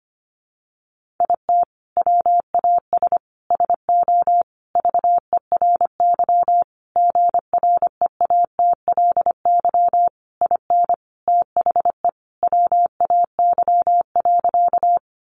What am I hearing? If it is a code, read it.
Morse code, heard as a keyed tone, about 25 wpm, 700 Hz. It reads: IT WAS SO VERY GREATLY IN THE WAY.
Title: Sentences from Top 1400 Words 25wpm